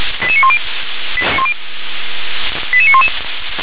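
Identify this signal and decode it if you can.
Five-tone selective call (ZVEI), heard as quick runs of five tones, tones of about 70 ms. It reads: 88910 88911
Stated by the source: ZVEI five-tone call received by SDR.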